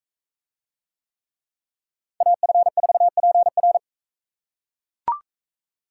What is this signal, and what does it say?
Morse code, heard as a keyed tone, about 40 wpm, 700 Hz. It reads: AF4PR